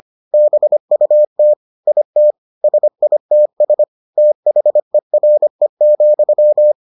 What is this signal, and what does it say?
Morse code, heard as a keyed tone, about 25 wpm, 600 Hz. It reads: BUT IT SITS THERE,